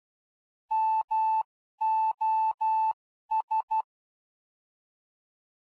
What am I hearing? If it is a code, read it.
Morse code, heard as a keyed tone, about 12 wpm, spaced as for 11 wpm, 880 Hz. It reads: MOS